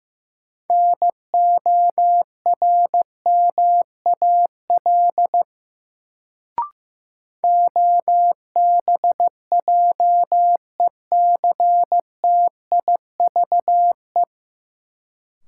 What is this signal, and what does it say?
Morse code, heard as a keyed tone, about 15 wpm, 700 Hz. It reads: NORMAL OBJECTIVE
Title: Sentences from Top 1200 Words - New Word Drill Alt - 15wpm